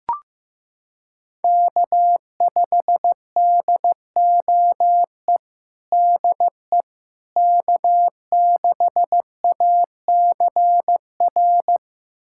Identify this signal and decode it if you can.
Morse code, heard as a keyed tone, about 14 wpm, 700 Hz. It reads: K5DOE DE K6ACR
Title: USA Callsign DE Callsign - 5 Characters - 15wpm